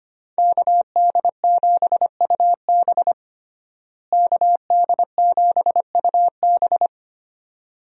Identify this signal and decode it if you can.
Morse code, heard as a keyed tone, about 25 wpm, 700 Hz. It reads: KD7UB KD7UB
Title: USA Call Signs - 5 Characters - Repeated 2x - 25wpm